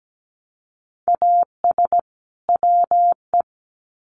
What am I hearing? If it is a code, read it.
Morse code, heard as a keyed tone, about 17 wpm, 700 Hz. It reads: AS WE
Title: Sets of 2 Words 17wpm